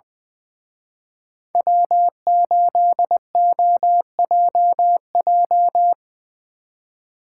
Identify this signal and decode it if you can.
Morse code, heard as a keyed tone, about 22 wpm, 700 Hz. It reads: W8OJJ
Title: Speed Racing - USA 1x3 Call Signs 20wpm